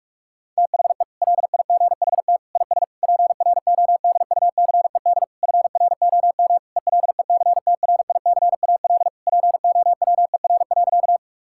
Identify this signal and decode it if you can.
Morse code, heard as a keyed tone, about 46 wpm, 700 Hz. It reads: THE LIGHT IS PRODUCED FROM ELECTRICAL POWER.
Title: Sentences from Top 1000 Words - Review Part 6 - 45wpm